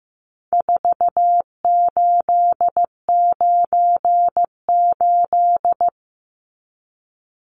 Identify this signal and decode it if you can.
Morse code, heard as a keyed tone, about 15 wpm, 700 Hz. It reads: 4898
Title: Numbers - 4-Digit 15wpm